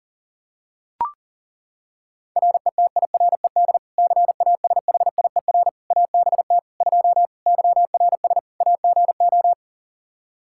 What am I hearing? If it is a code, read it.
Morse code, heard as a keyed tone, about 40 wpm, 700 Hz. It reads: RETIRED CASHIER ABT 2 YRS AGO
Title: K1USN SST QSO Practice 40wpm